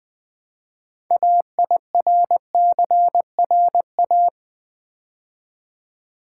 Morse code, heard as a keyed tone, about 20 wpm, 700 Hz. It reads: AIRCRA
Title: Build A Word - 2 Letters At A Time - 8-Letter Words - 20wpm